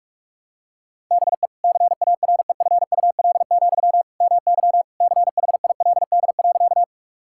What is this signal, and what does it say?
Morse code, heard as a keyed tone, about 41 wpm, 700 Hz. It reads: BE CAREFUL, MY CHILD.
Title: Sentences from Top 1600 Words 45wpm